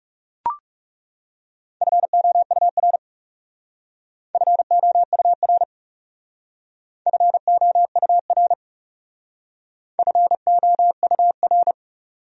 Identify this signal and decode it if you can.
Morse code, heard as a keyed tone, about 36 wpm, 700 Hz. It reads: FOUR FOUR FOUR FOUR
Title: Speed-Racing Top 300 Words 30wpm